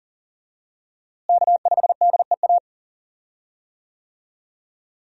Morse code, heard as a keyed tone, about 40 wpm, 700 Hz. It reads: K5DEA
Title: USA Call Signs - 5 Characters - Repeated 2x - 40wpm